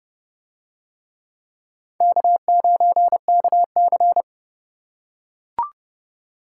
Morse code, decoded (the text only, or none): K9KC